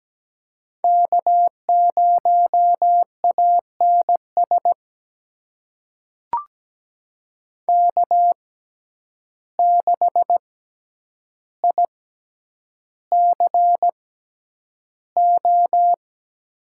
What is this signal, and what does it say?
Morse code, heard as a keyed tone, about 17 wpm, spaced as for 9 wpm, 700 Hz. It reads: K0ANS K6ICO
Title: USA Call Signs - 5 Characters - 6x Character Spacing - 17wpm